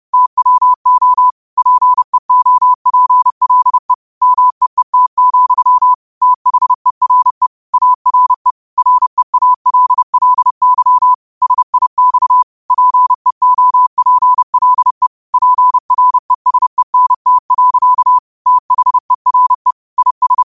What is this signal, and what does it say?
Morse code, heard as a keyed tone, about 30 wpm, 1000 Hz. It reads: TWO PEOPLE MEET, THERE ARE REALLY SIX PEOPLE PRESENT. THERE IS
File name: qotdfiles / 30wpm